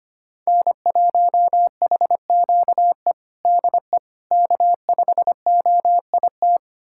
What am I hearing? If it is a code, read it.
Morse code, heard as a keyed tone, about 25 wpm, 700 Hz. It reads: N1HQE DE K5OIT
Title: USA Callsign DE Callsign - 5 Characters - 25wpm